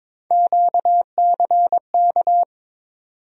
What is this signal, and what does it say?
Morse code, heard as a keyed tone, about 22 wpm, 700 Hz. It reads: QCK
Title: USA Call Sign Postfixes - 3 Letters 22wpm